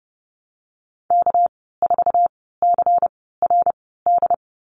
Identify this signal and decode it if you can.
Morse code, heard as a keyed tone, about 30 wpm, spaced as for 17 wpm, 700 Hz. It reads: K4CRD
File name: Lesson 37g - Letter x - Call Signs 30wpm